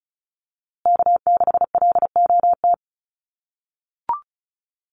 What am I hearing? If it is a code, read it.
Morse code, heard as a keyed tone, about 34 wpm, 700 Hz. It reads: K6LOT
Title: USA Call Signs Repeated 2x 35wpm